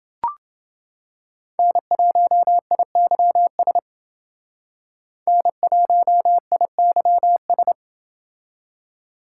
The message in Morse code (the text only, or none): N1IYS N1IYS